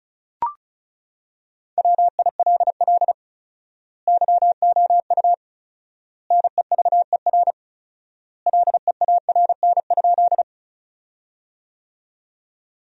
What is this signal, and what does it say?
Morse code, heard as a keyed tone, about 34 wpm, 700 Hz. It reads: WILL YOU NEVER LEARN?
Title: 4x Word-Spacing - Sentences from Top 300 Words - 35wpm